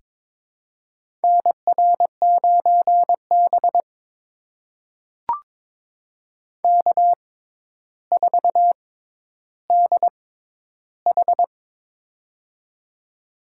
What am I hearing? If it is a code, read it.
Morse code, heard as a keyed tone, about 22 wpm, spaced as for 11 wpm, 700 Hz. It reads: NR9B K4DH
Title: USA Call Signs - 4 Characters - 6x Character Spacing - 22wpm